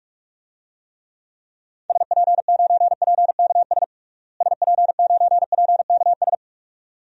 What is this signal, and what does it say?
Morse code, heard as a keyed tone, about 41 wpm, 700 Hz. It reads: SP9PKS SP9PKS
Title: DXCC-100 Call Signs - 6 Characters - Repeated 2x - 45wpm